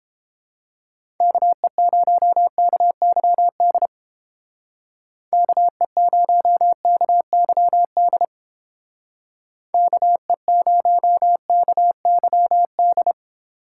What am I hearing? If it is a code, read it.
Morse code, heard as a keyed tone, about 29 wpm, 700 Hz. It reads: KE0KYD KE0KYD KE0KYD